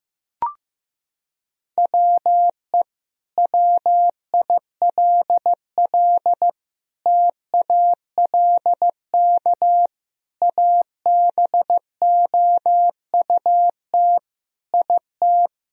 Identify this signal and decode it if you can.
Morse code, heard as a keyed tone, about 15 wpm, 700 Hz. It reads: WE WILL TALK ABOUT IT
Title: Sentences from Top 1000 Words - Review Part 4 - 15wpm